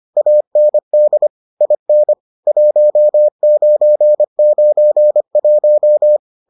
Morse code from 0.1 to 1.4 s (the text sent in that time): AND